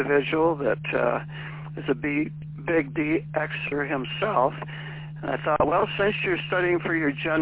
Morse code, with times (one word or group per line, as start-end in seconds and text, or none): none